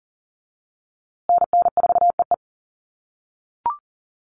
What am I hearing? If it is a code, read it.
Morse code, heard as a keyed tone, about 38 wpm, 700 Hz. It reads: NN4EE
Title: USA Call Signs Repeated 2x 40wpm